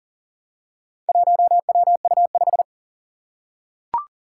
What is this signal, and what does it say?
Morse code, heard as a keyed tone, about 44 wpm, 690 Hz. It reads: 1WU5
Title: Four Challenging Letter-Number Combinations 40wpm